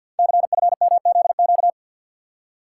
Morse code, heard as a keyed tone, about 50 wpm, 700 Hz. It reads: CFMZQ
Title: Code Groups - 5 Letters - 50wpm